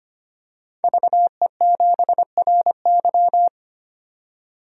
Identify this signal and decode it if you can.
Morse code, heard as a keyed tone, about 25 wpm, 700 Hz. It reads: VE7RY
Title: DXCC-100 Call Signs - 5 Characters - 25wpm